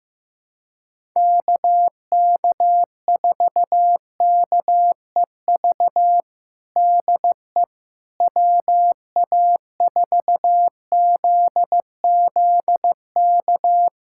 Morse code, heard as a keyed tone, about 15 wpm, 700 Hz. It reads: KK4KEV DE WA4ZZK
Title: USA Callsign DE Callsign - 6 Characters - 15wpm